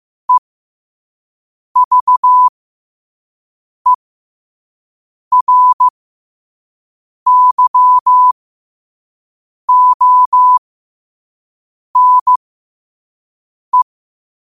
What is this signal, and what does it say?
Morse code, heard as a keyed tone, about 15 wpm, spaced as for 5 wpm, 1000 Hz. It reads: EVERYONE